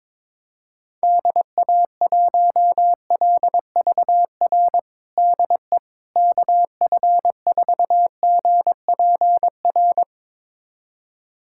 Morse code, heard as a keyed tone, about 22 wpm, 700 Hz. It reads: DA1LVR DE KF4GPR
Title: USA Callsign DE Callsign - 6 Characters - 22wpm